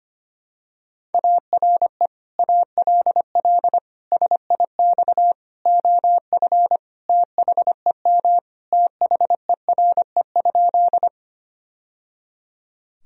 Morse code, heard as a keyed tone, about 25 wpm, 700 Hz. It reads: ARE ALL SIX OF THEM THERE?
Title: Sentences from Top 1000 Words - Review Part 2 - 25wpm